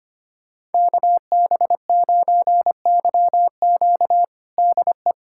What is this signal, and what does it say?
Morse code, heard as a keyed tone, about 25 wpm, 700 Hz. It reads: KB9YQ DE
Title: USA Callsign DE Callsign 25wpm